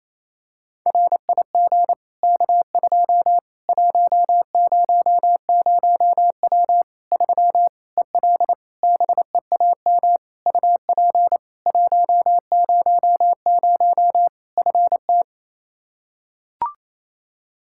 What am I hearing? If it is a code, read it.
Morse code, heard as a keyed tone, about 27 wpm, 700 Hz. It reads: RIG K2 100W 3 EL BEAM UP 100 FT
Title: K1USN SST QSO Practice 28wpm